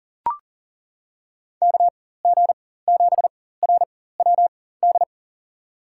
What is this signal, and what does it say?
Morse code, heard as a keyed tone, about 40 wpm, spaced as for 19 wpm, 700 Hz. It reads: KG7RWD
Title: USA Call Signs - 4x Character Spacing - 40wpm